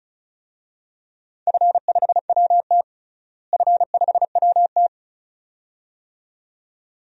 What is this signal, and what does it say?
Morse code, heard as a keyed tone, about 34 wpm, 700 Hz. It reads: F5WT F5WT
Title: DXCC-100 Call Signs - 4 Characters - Repeated 2x - 35wpm